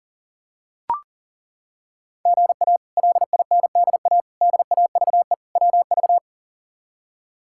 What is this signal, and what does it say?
Morse code, heard as a keyed tone, about 40 wpm, 700 Hz. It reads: GA LINDA DAVE WV
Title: K1USN SST QSO Practice - 40wpm